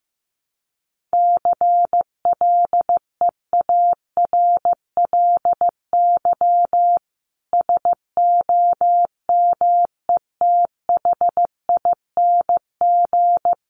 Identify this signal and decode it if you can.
Morse code, heard as a keyed tone, about 15 wpm, 700 Hz. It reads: CLEARLY SOMETHING
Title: Sentences from Top 1300 Words 15wpm